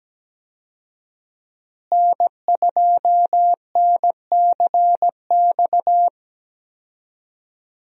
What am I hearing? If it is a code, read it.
Morse code, heard as a keyed tone, about 17 wpm, 700 Hz. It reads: N2NCX
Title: USA Call Signs - 5 Characters - Repeated 2x - 17wpm